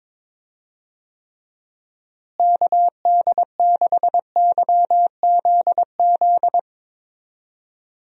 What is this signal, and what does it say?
Morse code, heard as a keyed tone, about 22 wpm, 700 Hz. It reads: KD6YZZ
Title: USA Call Signs - 6 Characters - Repeated 3x - 22wpm